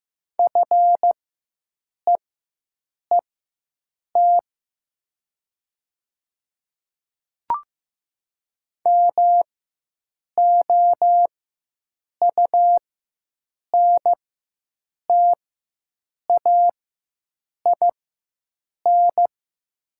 Morse code, heard as a keyed tone, about 15 wpm, spaced as for 7 wpm, 700 Hz. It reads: FEET MOUNTAIN
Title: 4x Character Spacing - Top 300 Words - 15wpm